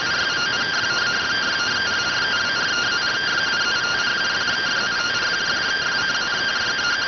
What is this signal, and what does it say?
Two-tone teleprinter signal, about 170 Hz shift.